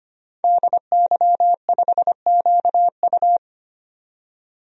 Morse code, heard as a keyed tone, about 25 wpm, 700 Hz. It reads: DY5QU
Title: DXCC-100 Call Signs - 5 Characters - 25wpm